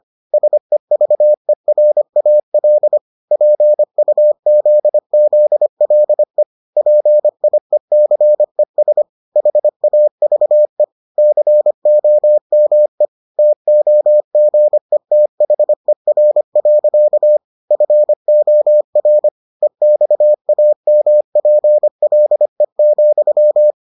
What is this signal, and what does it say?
Morse code, heard as a keyed tone, about 25 wpm, 600 Hz. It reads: SEVERAL PUZZLE PIECES HAVE COME TOGETHER. FOR EXAMPLE,